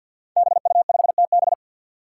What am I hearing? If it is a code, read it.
Morse code, heard as a keyed tone, about 53 wpm, 700 Hz. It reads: BU5TB
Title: Five Challenging Letter-Number Combinations 50wpm